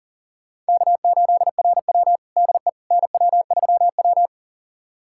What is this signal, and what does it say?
Morse code, heard as a keyed tone, about 37 wpm, 700 Hz. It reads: K8RW DE NW3W
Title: USA Callsign DE Callsign - 4 Characters - 40wpm